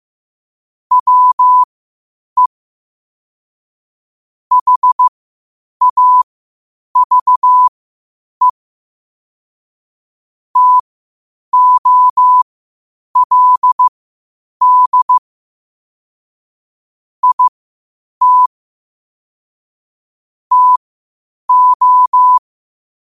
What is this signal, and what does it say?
Morse code, heard as a keyed tone, about 15 wpm, spaced as for 8 wpm, 1000 Hz. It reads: WE HAVE TOLD IT TO
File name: qotdfiles / 8wpm